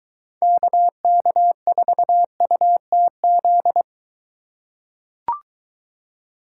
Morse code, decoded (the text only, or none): KK4UTZ